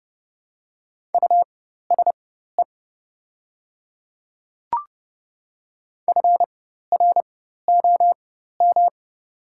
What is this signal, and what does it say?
Morse code, heard as a keyed tone, about 30 wpm, spaced as for 14 wpm, 700 Hz. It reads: USE FROM
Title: 4x Character Spacing - Top 200 Words - 30wpm